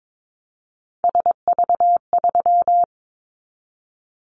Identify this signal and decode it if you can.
Morse code, heard as a keyed tone, about 22 wpm, 700 Hz. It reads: SV3
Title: Callsign Prefixes for Top 50 DXCC - 22wpm